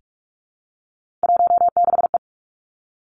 Morse code, heard as a keyed tone, about 46 wpm, 700 Hz. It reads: 16E
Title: Three Challenging Letter-Number Combinations 45wpm